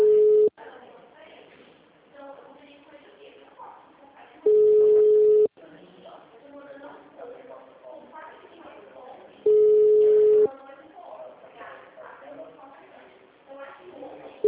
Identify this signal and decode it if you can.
Ringback tone (Europe, 425 Hz), with speech.
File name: out- -2002- - - .8952